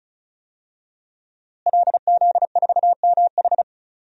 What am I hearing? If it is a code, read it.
Morse code, heard as a keyed tone, about 34 wpm, 700 Hz. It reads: LZ4MH
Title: DXCC-100 Call Signs - 5 Characters - Repeated 3x - 35wpm